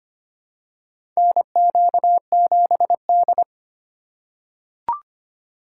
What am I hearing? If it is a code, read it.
Morse code, heard as a keyed tone, about 25 wpm, 700 Hz. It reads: NQ7D
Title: USA Call Signs - 4 Characters - 3x Character Spacing - 25wpm